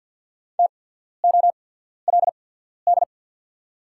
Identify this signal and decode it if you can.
Morse code, heard as a keyed tone, about 48 wpm, spaced as for 14 wpm, 690 Hz. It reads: TOLD
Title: 8x Character Spacing - Top 500 Words - 50wpm